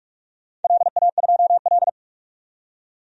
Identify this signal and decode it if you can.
Morse code, heard as a keyed tone, about 41 wpm, 700 Hz. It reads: LA2L
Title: DXCC-100 Call Signs - 4 Characters - 45wpm